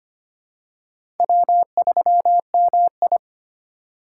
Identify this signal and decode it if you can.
Morse code, heard as a keyed tone, about 25 wpm, 700 Hz. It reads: W3MI
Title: USA Call Signs - 4 Characters - 3x Character Spacing - 25wpm